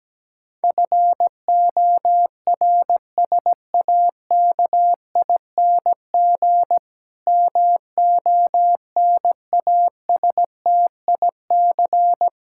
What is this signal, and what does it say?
Morse code, heard as a keyed tone, about 17 wpm, 700 Hz. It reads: FORSAKING MONASTIC
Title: Pangram Challenge v2 - 17wpm